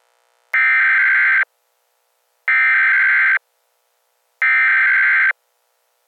Emergency Alert System (SAME) tones (header bursts), heard as harsh buzzing bursts. It is heard from 0.5 to 5.3 s.